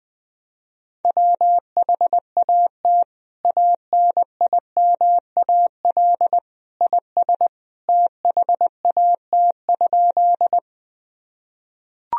Morse code, heard as a keyed tone, about 21 wpm, 700 Hz. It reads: WHAT ANIMAL IS THAT?
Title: Sentences from Top 1000 Words - Review Part 1 - 20wpm